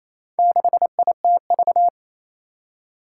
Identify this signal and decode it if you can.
Morse code, heard as a keyed tone, about 28 wpm, 700 Hz. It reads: 6ITV
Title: Four Challenging Letter-Number Combinations 28wpm